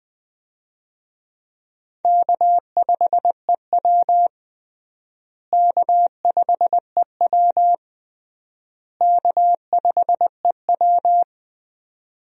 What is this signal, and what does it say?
Morse code, heard as a keyed tone, about 20 wpm, 700 Hz. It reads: K5EW K5EW K5EW
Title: USA Call Signs - 4 Characters - Repeated 3x - 20wpm